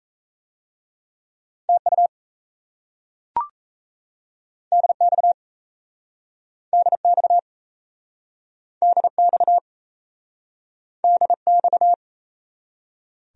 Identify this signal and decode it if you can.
Morse code, heard as a keyed tone, about 38 wpm, 700 Hz. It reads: TU DX DX DX DX